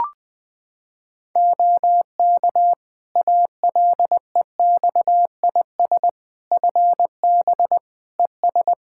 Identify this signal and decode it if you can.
Morse code, heard as a keyed tone, about 20 wpm, 700 Hz. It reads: OK ALEXIS FB ES
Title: Practice QSO - Exchange 2 - Info - 20wpm